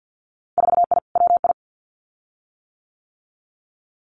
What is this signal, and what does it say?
Morse code, heard as a keyed tone, about 50 wpm, 700 Hz. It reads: 4I WI
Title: Winter Field Day Exchanges 50wpm